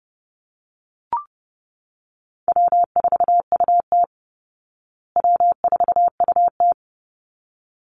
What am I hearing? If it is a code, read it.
Morse code, heard as a keyed tone, about 30 wpm, 700 Hz. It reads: W4UT W4UT
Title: Easy Call Signs Repeated 2x - 30wpm